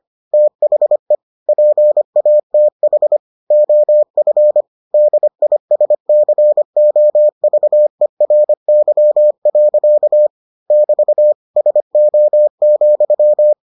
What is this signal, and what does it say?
Morse code, heard as a keyed tone, about 25 wpm, 600 Hz. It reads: THE PATH OF DISCOVERY. <BT> SO,